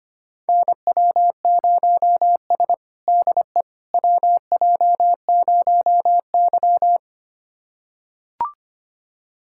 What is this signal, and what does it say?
Morse code, heard as a keyed tone, about 25 wpm, 700 Hz. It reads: NW0S DE WJ0Y
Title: USA Callsign DE Callsign - 4 Characters - 25wpm